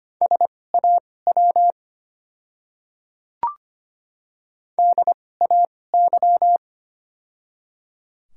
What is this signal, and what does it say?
Morse code, heard as a keyed tone, about 25 wpm, spaced as for 18 wpm, 700 Hz. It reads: SAW DAY
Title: 2x Character Spacing - Top 500 Words - 25wpm